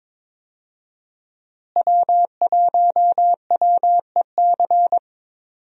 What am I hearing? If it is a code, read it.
Morse code, heard as a keyed tone, about 22 wpm, 700 Hz. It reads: W1WEC
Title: USA Call Signs - 5 Characters - Repeated 3x - 22wpm